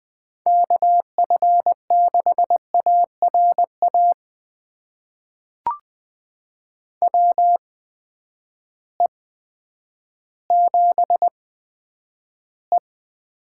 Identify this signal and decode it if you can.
Morse code, heard as a keyed tone, about 20 wpm, spaced as for 10 wpm, 700 Hz. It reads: KF6ARA WE7E
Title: USA Call Signs - 8x Character Spacing - 20wpm